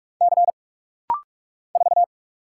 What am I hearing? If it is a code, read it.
Morse code, heard as a keyed tone, about 44 wpm, 700 Hz. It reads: C 4
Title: Single Letter-Number Alt - Rapid-Fire 45wpm